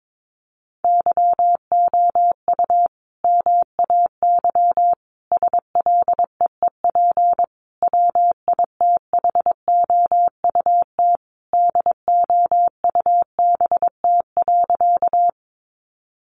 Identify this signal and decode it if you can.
Morse code, heard as a keyed tone, about 22 wpm, 700 Hz. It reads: YOU MAY SLEEP WITHOUT DOUBT.
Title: Sentences from Top 1500 Words 22wpm